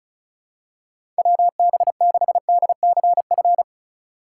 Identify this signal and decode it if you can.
Morse code, heard as a keyed tone, about 35 wpm, 700 Hz. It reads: WB6DCF
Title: USA Call Signs - 6 Characters - 6x Character Spacing - 35wpm